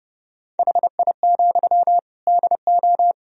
Morse code, heard as a keyed tone, about 31 wpm, 700 Hz. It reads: HI, DO
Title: Sentences from Top 1000 Words - Review Part 3 - 30wpm